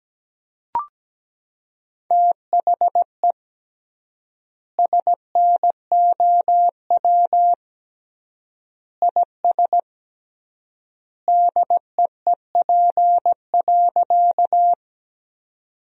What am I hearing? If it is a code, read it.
Morse code, heard as a keyed tone, about 17 wpm, 700 Hz. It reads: THE SNOW IS DEEP.
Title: 3x Word-Spacing - Sentences from Top 500 Words - 17wpm